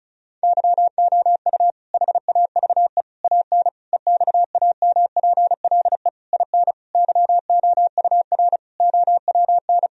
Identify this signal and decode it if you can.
Morse code, heard as a keyed tone, about 36 wpm, 700 Hz. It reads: YOU HAVE AN EXAMPLE IN YOUR OWN
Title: Sentences from Top 1000 Words - Review Part 1 - 35wpm